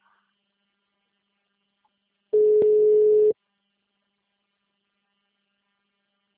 Ringback tone (Europe, 425 Hz).
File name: out- -2033- - - .23775